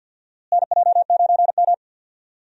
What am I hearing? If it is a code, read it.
Morse code, heard as a keyed tone, about 50 wpm, 700 Hz. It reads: NJ9M
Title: USA Call Signs - 4 Characters - Repeated 3x - 50wpm